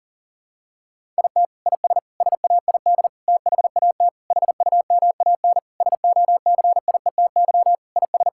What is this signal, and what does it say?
Morse code, heard as a keyed tone, about 40 wpm, 700 Hz. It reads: IT IS SAID THAT HUMAN SOCIETY IS
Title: Sentences from Top 1000 Words - Review Part 6 - 40wpm